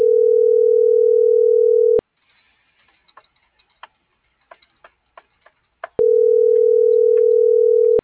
Ringback tone (North America), 440 + 480 Hz.